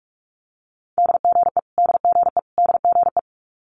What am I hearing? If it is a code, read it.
Morse code, heard as a keyed tone, about 46 wpm, 700 Hz. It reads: DGE DGE DGE
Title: Multi-letter Phonemes Repeated 3x 45wpm